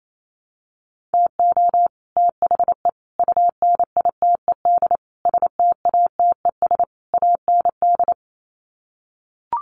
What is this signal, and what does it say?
Morse code, heard as a keyed tone, about 28 wpm, 700 Hz. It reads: TO THE UNITED STATES AND
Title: Sets of 5 Words Encore 28wpm